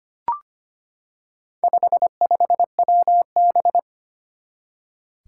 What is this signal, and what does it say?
Morse code, heard as a keyed tone, about 25 wpm, 700 Hz. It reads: 55WB
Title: Four Challenging Letter-Number Combinations 25wpm